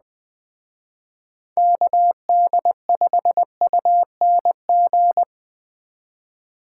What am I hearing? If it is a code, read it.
Morse code, heard as a keyed tone, about 20 wpm, 700 Hz. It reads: KD5UNG
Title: Speed Racing - USA 2x3 Call Signs 17wpm